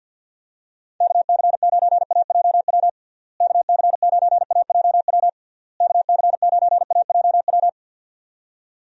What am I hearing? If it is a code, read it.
Morse code, heard as a keyed tone, about 50 wpm, 700 Hz. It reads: KC9AJW KC9AJW KC9AJW
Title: USA Call Signs - 6 Characters - Repeated 3x - 50wpm